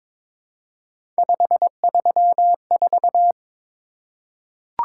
Morse code, heard as a keyed tone, about 23 wpm, 700 Hz. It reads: 534
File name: practice / 22wpm / Three Letter-Number Combinations - 22wpm